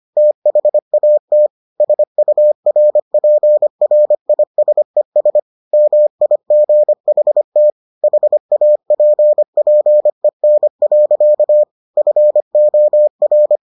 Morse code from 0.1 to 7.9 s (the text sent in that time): THAT SURPRISES MIGHT